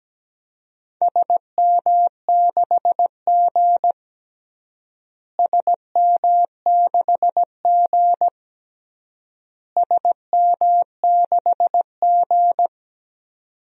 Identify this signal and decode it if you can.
Morse code, heard as a keyed tone, about 17 wpm, 700 Hz. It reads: SM6G SM6G SM6G
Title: DXCC-100 Call Signs - 4 Characters - Repeated 3x - 17wpm